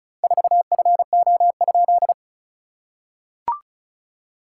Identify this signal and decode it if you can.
Morse code, heard as a keyed tone, about 34 wpm, 700 Hz. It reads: 4FO?
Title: SOTA QSO Practice 35wpm